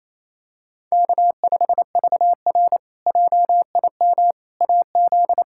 Morse code, heard as a keyed tone, about 28 wpm, 700 Hz. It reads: K5VR JIM AZ
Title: K1USN SST QSO Practice - 28wpm